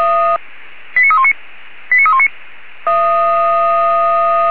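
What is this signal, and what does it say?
Five-tone selective call (ZVEI), heard as quick runs of five tones, tones of about 70 ms. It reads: 89319 89319
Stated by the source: ZVEI five-tone selective call 89315 received on SDR, from feuerwehr-orlamuende fire brigade server.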